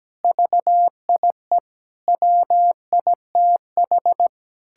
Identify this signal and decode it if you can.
Morse code, heard as a keyed tone, about 17 wpm, 700 Hz. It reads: VIE WITH
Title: Pangram Challenge v2 - 17wpm